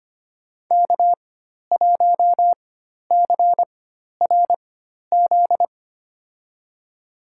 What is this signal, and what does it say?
Morse code, heard as a keyed tone, about 25 wpm, spaced as for 12 wpm, 700 Hz. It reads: K1CRZ
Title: USA Call Signs - 5 Characters - 4x Character Spacing - 25wpm